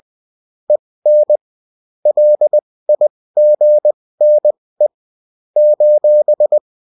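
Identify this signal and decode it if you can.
Morse code, heard as a keyed tone, about 20 wpm, spaced as for 16 wpm, 600 Hz. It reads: EN LIGNE :
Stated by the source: Morse code bulletin of a French amateur CW union, sent as text and in CW.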